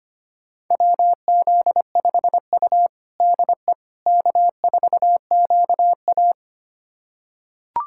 Morse code, heard as a keyed tone, about 25 wpm, 700 Hz. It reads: WZ5U DE K4QA
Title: USA Callsign DE Callsign - 4 Characters - 25wpm